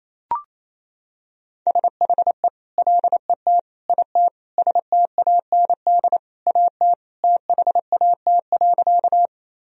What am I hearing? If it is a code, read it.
Morse code, heard as a keyed tone, about 28 wpm, 700 Hz. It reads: SHE LET IT STAND AT THAT.
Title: Sentences from Top 1000 Words - Review Part 1 - 28wpm